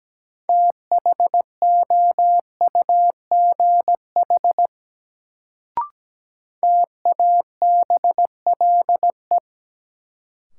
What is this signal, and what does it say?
Morse code, heard as a keyed tone, about 17 wpm, 700 Hz. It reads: THOUGH TABLE
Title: Sentences from Top 400 Words - New Word Drill - 17wpm